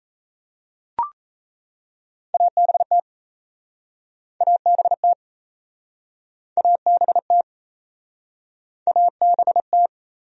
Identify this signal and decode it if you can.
Morse code, heard as a keyed tone, about 38 wpm, 700 Hz. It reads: ABT ABT ABT ABT